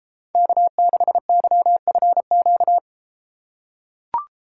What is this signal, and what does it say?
Morse code, heard as a keyed tone, about 31 wpm, 700 Hz. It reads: K6YFQ